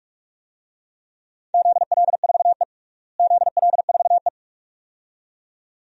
Morse code, heard as a keyed tone, about 42 wpm, 700 Hz. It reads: ZL4E ZL4E
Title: DXCC-100 Call Signs - 4 Characters - Repeated 2x - 45wpm